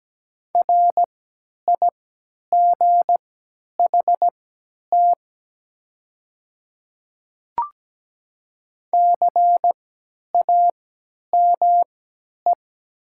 Morse code, heard as a keyed tone, about 17 wpm, spaced as for 10 wpm, 700 Hz. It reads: RIGHT CAME
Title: 3x Character Spacing - Top 200 Words - 17wpm